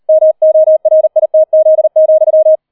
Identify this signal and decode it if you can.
Morse code, heard as a keyed tone, about 38 wpm, 630 Hz. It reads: MORITZ,
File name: 003 Dritter Streich 40wpm